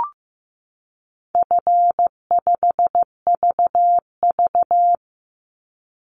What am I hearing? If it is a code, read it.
Morse code, heard as a keyed tone, about 15 wpm, 700 Hz. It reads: F5VV
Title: Easy Call Signs Repeated 3x - 15wpm